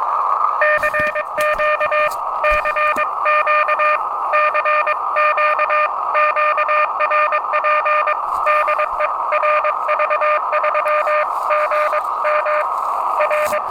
Morse code, heard as a keyed tone, about 22 wpm, 600 Hz, with radio noise. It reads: CQ CQ CQ QRP DE RV3GM R
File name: attachments / RV3GM CQ